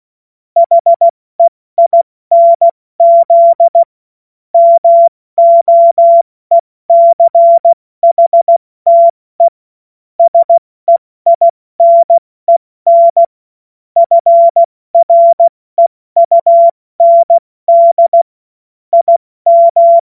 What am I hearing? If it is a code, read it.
Morse code, heard as a keyed tone, about 16 wpm, 680 Hz. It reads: HEINZ MOECHTE SEINEN FREUND IM
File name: Morsekurs / Witze / Witz 20. Dezember 2022 70 F80